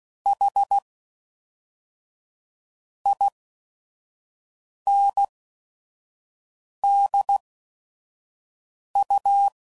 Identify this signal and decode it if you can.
Morse code, heard as a keyed tone, about 16 wpm, 800 Hz. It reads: H I N D U